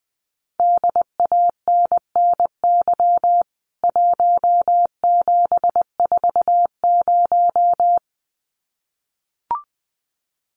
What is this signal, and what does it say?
Morse code, heard as a keyed tone, about 20 wpm, 700 Hz. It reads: DANNY 1740
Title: CWT Exchange 20wpm